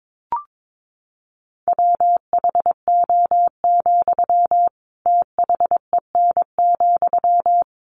Morse code, heard as a keyed tone, about 22 wpm, 700 Hz. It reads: WHO, THEN,
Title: Sentences from Top 100 Words 22wpm